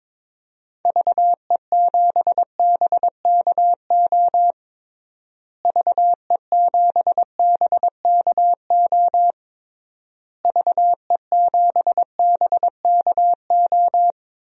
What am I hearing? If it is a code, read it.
Morse code, heard as a keyed tone, about 22 wpm, 700 Hz. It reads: VE7BKO VE7BKO VE7BKO
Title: DXCC-100 Call Signs - 6 Characters - Repeated 3x - 22wpm